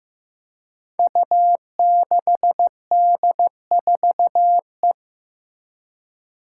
Morse code, heard as a keyed tone, about 15 wpm, 700 Hz. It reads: U6D4E
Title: Five Challenging Letter-Number Combinations 15wpm